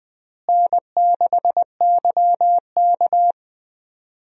Morse code, heard as a keyed tone, about 20 wpm, 700 Hz. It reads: N6YK
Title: USA Call Signs - 4 Characters - Repeated 3x - 20wpm